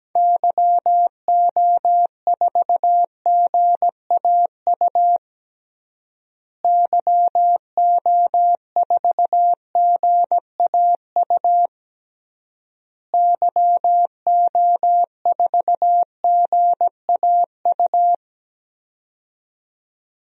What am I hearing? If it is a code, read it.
Morse code, heard as a keyed tone, about 17 wpm, 700 Hz. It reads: YO4GAU YO4GAU YO4GAU
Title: DXCC-100 Call Signs - 6 Characters - Repeated 3x - 17wpm